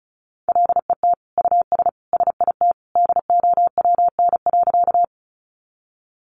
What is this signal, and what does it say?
Morse code, heard as a keyed tone, about 34 wpm, 700 Hz. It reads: LET US SIT DOWN.
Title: Sentences from Top 500 Words 35wpm